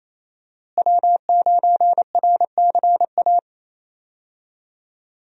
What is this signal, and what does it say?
Morse code, heard as a keyed tone, about 27 wpm, 700 Hz. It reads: W9RCA
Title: USA Call Signs - 5 Characters - Repeated 2x - 28wpm